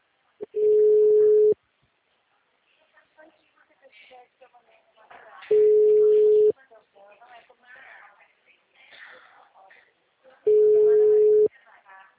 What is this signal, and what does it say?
Ringback tone (Europe, 425 Hz), with speech.
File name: out- -2012